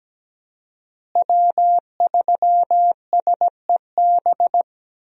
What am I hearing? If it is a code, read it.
Morse code, heard as a keyed tone, about 17 wpm, 700 Hz. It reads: W3SEB
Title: USA Call Signs - 5 Characters - Repeated 2x - 17wpm